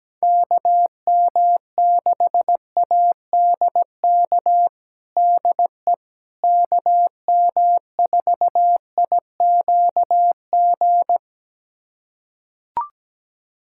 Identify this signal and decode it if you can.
Morse code, heard as a keyed tone, about 17 wpm, 700 Hz. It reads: KM6ADK DE KM4IQG
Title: USA Callsign DE Callsign - 6 Characters - 17wpm